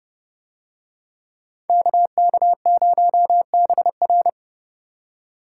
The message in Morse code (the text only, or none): KK0BR